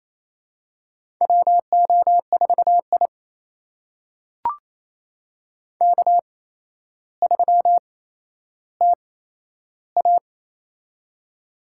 Morse code, heard as a keyed tone, about 28 wpm, spaced as for 12 wpm, 700 Hz. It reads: WO4I K3TA
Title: USA Call Signs - 4 Characters - 8x Character Spacing - 28wpm